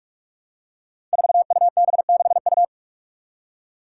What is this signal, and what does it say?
Morse code, heard as a keyed tone, about 42 wpm, 700 Hz. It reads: 4UB6U
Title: Five Challenging Letter-Number Combinations 45wpm